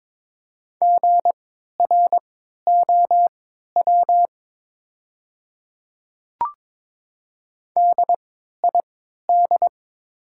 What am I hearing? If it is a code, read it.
Morse code, heard as a keyed tone, about 22 wpm, spaced as for 13 wpm, 700 Hz. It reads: GROW DID
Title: 3x Character Spacing - Top 300 Words - 22wpm